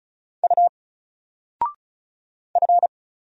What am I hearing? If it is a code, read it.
Morse code, heard as a keyed tone, about 34 wpm, 700 Hz. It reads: U F
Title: Single Character (Letter:Number:Punctuation:Prosign) 35wpm